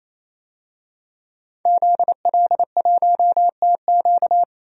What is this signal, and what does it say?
Morse code, heard as a keyed tone, about 28 wpm, 700 Hz. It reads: ZL1TQ
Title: DXCC-100 Call Signs - 5 Characters - 28wpm